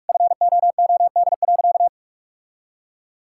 Morse code, heard as a keyed tone, about 46 wpm, 690 Hz. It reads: FOOD.
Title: Sentences from Top 1000 Words - Review Part 6 - 45wpm